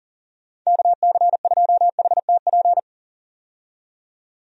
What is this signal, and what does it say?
Morse code, heard as a keyed tone, about 39 wpm, 700 Hz. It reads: KC2HTP
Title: USA Call Signs - 6 Characters - 40wpm